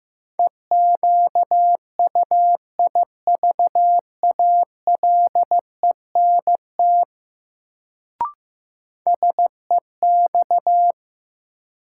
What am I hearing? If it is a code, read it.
Morse code, heard as a keyed tone, about 15 wpm, 700 Hz. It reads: EQUIVALENT SEX
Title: Sentences from Top 1600 Words - New Word Drill Alt - 15wpm